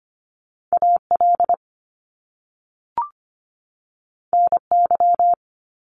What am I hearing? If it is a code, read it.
Morse code, heard as a keyed tone, about 25 wpm, 700 Hz. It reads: AL NY
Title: Top Two Letter Combinations in Words 25wpm